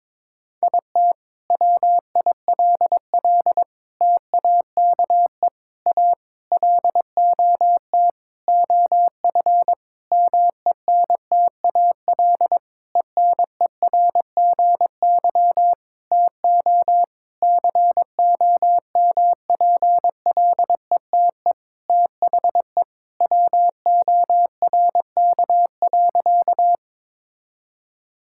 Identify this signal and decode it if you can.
Morse code, heard as a keyed tone, about 22 wpm, 700 Hz. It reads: IT WILL TAKE A LOT OF MENTAL ENERGY TO COMPLETE THE WORK.
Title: Sentences from Top 1000 Words - Review Part 6 - 22wpm